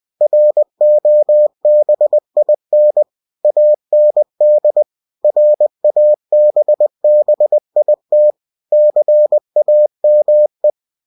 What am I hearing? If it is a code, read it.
Morse code, heard as a keyed tone, about 20 wpm, 600 Hz. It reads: ROBIN AND RABBIT CAME
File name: winnie-20wpm